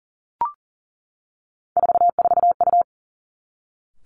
Morse code, heard as a keyed tone, about 38 wpm, 690 Hz. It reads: 44U
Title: Three Challenging Letter-Number Combinations 40wpm